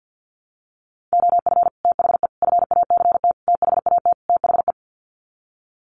Morse code, heard as a keyed tone, about 50 wpm, 700 Hz. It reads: OF THE FACT THAT THE
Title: Sets of 5 Words Encore2 50wpm